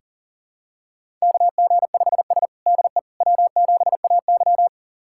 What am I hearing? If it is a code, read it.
Morse code, heard as a keyed tone, about 40 wpm, 700 Hz. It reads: KG5S DE W7AY
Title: USA Callsign DE Callsign - 4 Characters - 40wpm